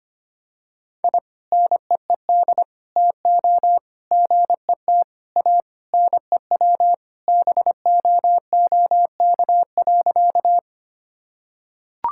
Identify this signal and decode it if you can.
Morse code, heard as a keyed tone, about 25 wpm, 700 Hz. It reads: I NEED TO GET A NEW BOOK.
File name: practice / 25wpm / Sentences with 3-Word Phrases - 25wpm